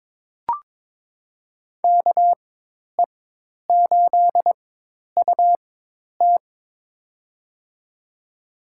Morse code, heard as a keyed tone, about 22 wpm, spaced as for 10 wpm, 700 Hz. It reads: KE8UT
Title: USA Call Signs - 5 Characters - 4x Character Spacing - 22wpm